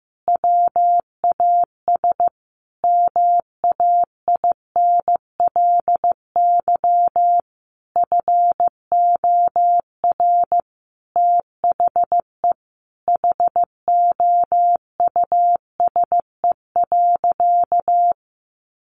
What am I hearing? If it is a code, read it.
Morse code, heard as a keyed tone, about 14 wpm, 700 Hz. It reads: WAS MAINLY FOR THE HOUSE.
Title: Sentences from Top 1400 Words 15wpm